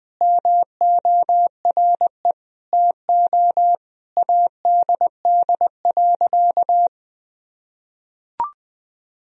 Morse code, heard as a keyed tone, about 20 wpm, 700 Hz. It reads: MORE TO ADD.
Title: Sentences from Top 1000 Words - Review Part 1 - 20wpm